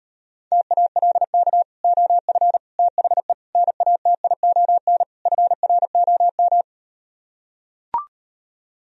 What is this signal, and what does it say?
Morse code, heard as a keyed tone, about 38 wpm, 700 Hz. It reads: TALK OF THE NATION FROM